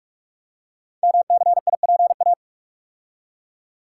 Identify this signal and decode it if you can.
Morse code, heard as a keyed tone, about 44 wpm, 690 Hz. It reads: MCIPA
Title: Code Groups - 5 Letters - 45wpm